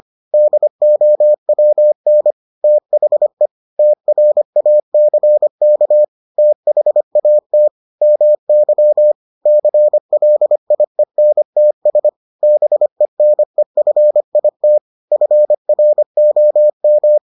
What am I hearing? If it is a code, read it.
Morse code, heard as a keyed tone, about 25 wpm, 600 Hz. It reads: DOWN THE TRACK THAT MY CLIENTS BENEFIT FROM